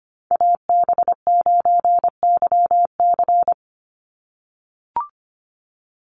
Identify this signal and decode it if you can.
Morse code, heard as a keyed tone, about 25 wpm, 700 Hz. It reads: AB9YC
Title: Call Signs - 25wpm